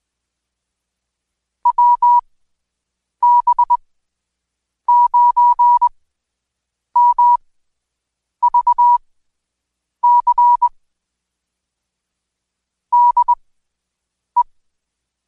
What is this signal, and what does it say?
Morse code, heard as a keyed tone, about 21 wpm, spaced as for 7 wpm, 970 Hz. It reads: WB9MVC DE
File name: Code Quick / AudioDownld2 / Seg113-Day18-DayDrill 18